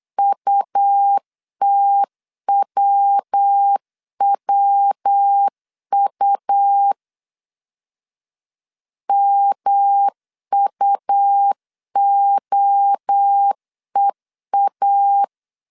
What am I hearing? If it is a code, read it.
Morse code, heard as a keyed tone, about 8 wpm, 800 Hz. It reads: UTWWU MUOEA
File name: CW / DET / 1 3 1500C S B 8wpm 4141